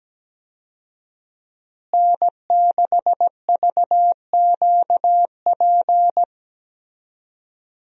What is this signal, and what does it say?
Morse code, heard as a keyed tone, about 17 wpm, 700 Hz. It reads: N6VQP
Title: USA Call Signs - 5 Characters - Repeated 3x - 17wpm